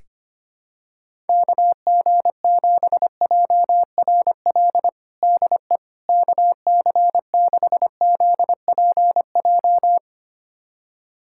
Morse code, heard as a keyed tone, about 25 wpm, 700 Hz. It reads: KG7JRL DE KC6ZPJ
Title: USA Callsign DE Callsign - 6 Characters - 25wpm